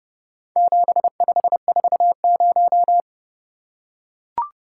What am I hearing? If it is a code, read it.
Morse code, heard as a keyed tone, about 31 wpm, 700 Hz. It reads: 7540